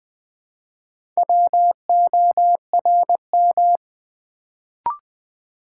Morse code, heard as a keyed tone, about 20 wpm, 700 Hz. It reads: WORM
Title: Build A Word - 1 Letter At A Time - 3 to 5-Letter Words - 20wpm